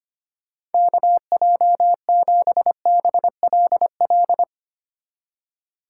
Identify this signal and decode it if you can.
Morse code, heard as a keyed tone, about 31 wpm, 700 Hz. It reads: KJ7BLL